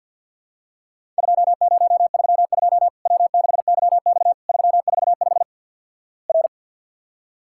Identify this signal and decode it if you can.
Morse code, heard as a keyed tone, about 50 wpm, 700 Hz. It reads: 2032 W6YX 345 R
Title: Worked All Europe DX Contest - QTC Exchange with Repeats - 50wpm